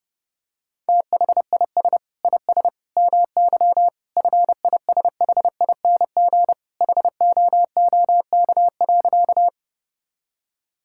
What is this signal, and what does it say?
Morse code, heard as a keyed tone, about 30 wpm, 700 Hz. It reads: THIS IS MY FISHING HOOK.
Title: Sentences from Top 1600 Words 30wpm